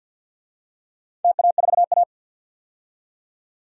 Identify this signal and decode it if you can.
Morse code, heard as a keyed tone, about 47 wpm, 700 Hz. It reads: TA4A
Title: DXCC-100 Call Signs - 4 Characters - 50wpm